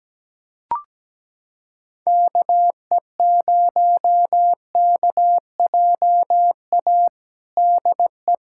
Morse code, heard as a keyed tone, about 17 wpm, 700 Hz. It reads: KE0KJA DE
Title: USA Callsign DE Callsign - 6 Characters - 17wpm